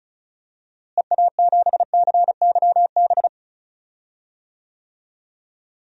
Morse code, heard as a keyed tone, about 36 wpm, 700 Hz. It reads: EA7CYB
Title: DXCC-100 Call Signs - 6 Characters - Repeated 3x - 35wpm